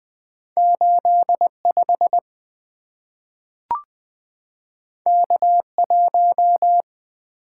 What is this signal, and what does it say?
Morse code, heard as a keyed tone, about 20 wpm, 700 Hz. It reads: 8 5 K 1
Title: Two Letter-Number Combinations - 20wpm